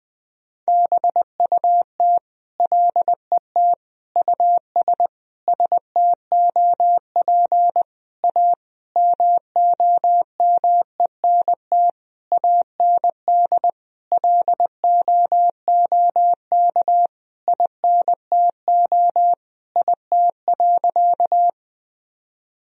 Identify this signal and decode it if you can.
Morse code, heard as a keyed tone, about 21 wpm, 700 Hz. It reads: BUT LET US STOP A MOMENT AND LOOK INTO IT.
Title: Sentences from Top 1000 Words - Review Part 1 - 20wpm